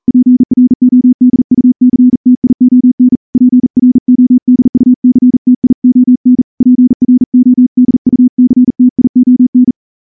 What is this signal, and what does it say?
Morse code, heard as a keyed tone, about 38 wpm, 250 Hz. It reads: PRODUCTION PRODUCTION PRODUCTION
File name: morse-training-40WPM